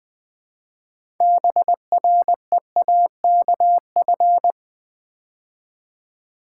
Morse code, heard as a keyed tone, about 20 wpm, 700 Hz. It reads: BREAKF
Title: Build A Word - 3 Letters At A Time - 9-Letter Words - 20wpm